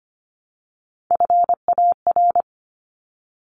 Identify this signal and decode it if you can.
Morse code, heard as a keyed tone, about 25 wpm, 700 Hz. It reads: FAR
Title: Top 300 Words 25wpm